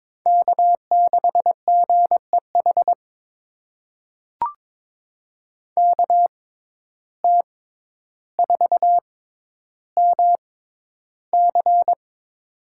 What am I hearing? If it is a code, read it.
Morse code, heard as a keyed tone, about 22 wpm, spaced as for 11 wpm, 700 Hz. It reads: K6GEH KT4MC
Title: USA Call Signs - 5 Characters - 6x Character Spacing - 22wpm